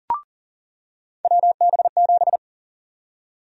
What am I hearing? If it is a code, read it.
Morse code, heard as a keyed tone, about 40 wpm, 700 Hz. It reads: WB7
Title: USA Call Sign Prefixes - 40wpm